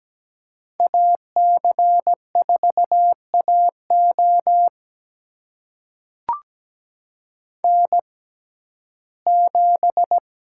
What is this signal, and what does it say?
Morse code, heard as a keyed tone, about 17 wpm, spaced as for 12 wpm, 700 Hz. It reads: AC4AO N7
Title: USA Call Signs - 5 Characters - 6x Character Spacing - 17wpm